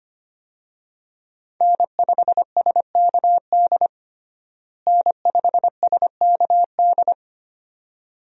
Morse code, heard as a keyed tone, about 25 wpm, 700 Hz. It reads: N5SKD N5SKD
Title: USA Call Signs - 5 Characters - Repeated 2x - 25wpm